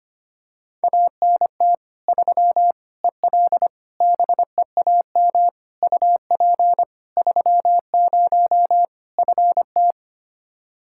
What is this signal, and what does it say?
Morse code, heard as a keyed tone, about 25 wpm, 700 Hz. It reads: ANT 3 EL BEAM UP 30 FT
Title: Practice QSO - Exchange 2 - Info - 25wpm